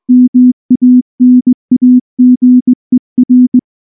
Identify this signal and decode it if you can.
Morse code, heard as a keyed tone, about 20 wpm, 250 Hz. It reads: MANAGER
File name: morse / morse-training-20WPM